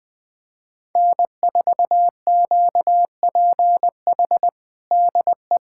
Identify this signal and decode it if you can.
Morse code, heard as a keyed tone, about 20 wpm, 700 Hz. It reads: N4QPH DE
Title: USA Callsign DE Callsign 20wpm